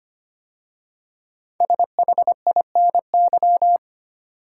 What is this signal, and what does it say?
Morse code, heard as a keyed tone, about 25 wpm, 700 Hz. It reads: SHINY
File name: 5 Letter Words - First 3 Letters Given - 25wpm